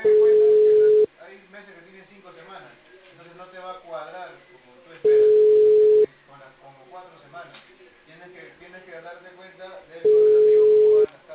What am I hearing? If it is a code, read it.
Ringback tone (Europe, 425 Hz), with speech.